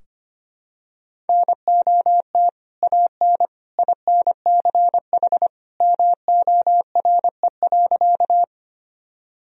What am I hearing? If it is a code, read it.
Morse code, heard as a keyed tone, about 25 wpm, 700 Hz. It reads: NOT AN INCH MORE.
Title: Sentences from Top 1000 Words - Review Part 3 - 25wpm